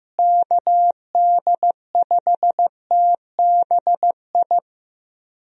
Morse code, heard as a keyed tone, about 15 wpm, 700 Hz. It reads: KD5TBI
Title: USA Call Signs - 6 Characters - Repeated 2x - 15wpm